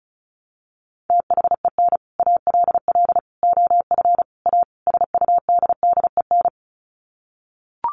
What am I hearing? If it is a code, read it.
Morse code, heard as a keyed tone, about 34 wpm, 700 Hz. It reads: THEN ALL OF A SUDDEN
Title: Sets of 5 Words Encore 35wpm